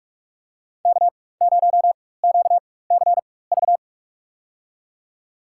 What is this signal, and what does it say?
Morse code, heard as a keyed tone, about 46 wpm, spaced as for 21 wpm, 700 Hz. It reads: K0QCV
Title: USA Call Signs - 5 Characters - 4x Character Spacing - 45wpm